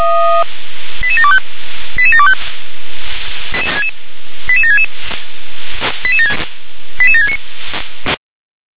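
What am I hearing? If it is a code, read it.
Five-tone selective call (ZVEI), heard as quick runs of five tones, tones of about 70 ms. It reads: 88725 88725 88766 88769
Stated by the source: ZVEI five-tone selective call received by SDR.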